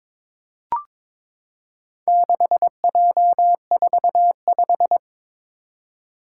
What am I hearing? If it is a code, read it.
Morse code, heard as a keyed tone, about 22 wpm, 700 Hz. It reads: 6J45
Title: Four Challenging Letter-Number Combinations 22wpm